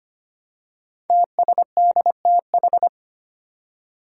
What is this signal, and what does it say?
Morse code, heard as a keyed tone, about 25 wpm, 700 Hz. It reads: TSDTH
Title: Five Challenging Letter-Number Combinations 25wpm